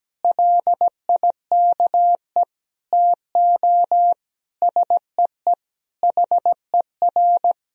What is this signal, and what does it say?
Morse code, heard as a keyed tone, about 17 wpm, 700 Hz. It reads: LIKE TO SEE HER
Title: Sentences from Top 200 Words and Abbreviations 17wpm